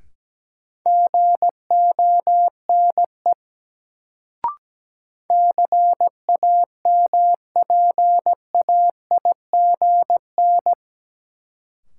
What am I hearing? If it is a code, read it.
Morse code, heard as a keyed tone, about 17 wpm, 700 Hz. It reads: GONE CAMPAIGN
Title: Sentences from Top 1200 Words - New Word Drill Alt - 17wpm